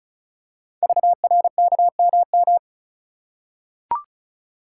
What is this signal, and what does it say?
Morse code, heard as a keyed tone, about 36 wpm, 690 Hz. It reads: VRKMM
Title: Code Groups - 5 Letters - 35wpm